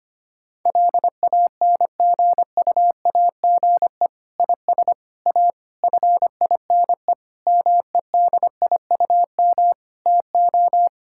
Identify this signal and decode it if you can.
Morse code, heard as a keyed tone, about 25 wpm, 700 Hz. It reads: LANGUAGE IS A FINE MEDIUM TO
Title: Sentences from Top 1000 Words - Review Part 6 - 25wpm